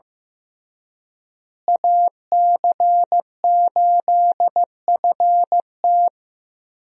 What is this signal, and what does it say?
Morse code, heard as a keyed tone, about 16 wpm, 700 Hz. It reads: AC8FT
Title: USA Call Signs - 5 Characters - Repeated 3x - 15wpm